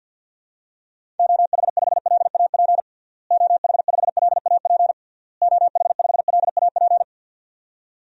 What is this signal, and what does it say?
Morse code, heard as a keyed tone, about 50 wpm, 700 Hz. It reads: OH5LAP OH5LAP OH5LAP
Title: DXCC-100 Call Signs - 6 Characters - Repeated 3x - 50wpm